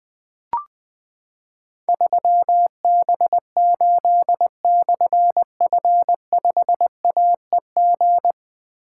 Morse code, heard as a keyed tone, about 20 wpm, 700 Hz. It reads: 3B8/F5AEG
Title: Difficult Call Signs 20wpm